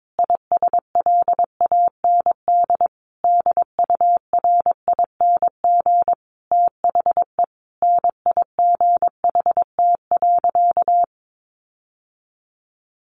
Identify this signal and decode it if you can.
Morse code, heard as a keyed tone, about 22 wpm, 700 Hz. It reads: ISLAND DURING THE NIGHT.
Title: Sentences from Top 1400 Words 22wpm